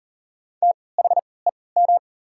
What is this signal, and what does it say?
Morse code, heard as a keyed tone, about 40 wpm, spaced as for 23 wpm, 690 Hz. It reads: THEM
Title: 3x Character Spacing - Top 100 Words - 40wpm